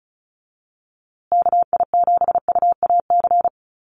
Morse code, heard as a keyed tone, about 34 wpm, 700 Hz. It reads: KI7UAC
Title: USA Call Signs Repeated 3x 35wpm